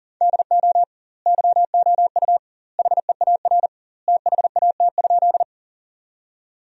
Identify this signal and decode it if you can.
Morse code, heard as a keyed tone, about 40 wpm, 700 Hz. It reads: DO YOU HEAR THAT?
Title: 2x Word-Spacing - Sentences from Top 300 Words - 40wpm